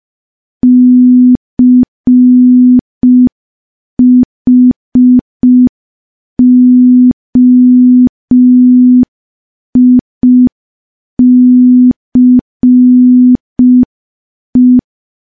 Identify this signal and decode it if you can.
Morse code, heard as a keyed tone, about 5 wpm, 250 Hz. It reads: CHOICE
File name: morse / morse-training-05WPM